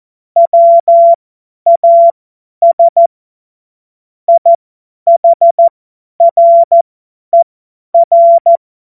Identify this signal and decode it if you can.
Morse code, heard as a keyed tone, about 14 wpm, spaced as for 10 wpm, 680 Hz. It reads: WAS IHRER
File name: Morsekurs / Witze / Witz 10.Oktober 2023 50 F70